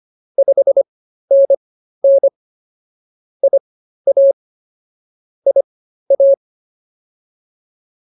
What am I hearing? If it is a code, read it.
Morse code, heard as a keyed tone, about 24 wpm, spaced as for 13 wpm, 550 Hz. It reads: 5NN IA IA